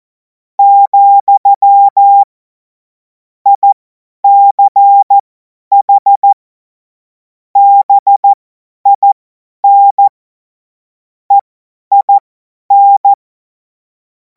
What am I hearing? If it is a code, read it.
Morse code, heard as a keyed tone, about 14 wpm, spaced as for 10 wpm, 800 Hz. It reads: , ICH BIN EIN
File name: Morsekurs / Witze / Witz 21. April 2026 50 F70